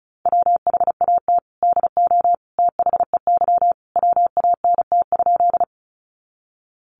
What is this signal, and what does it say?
Morse code, heard as a keyed tone, about 34 wpm, 700 Hz. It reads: WHAT DO THEY WANT?
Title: Sentences from Top 200 Words 35wpm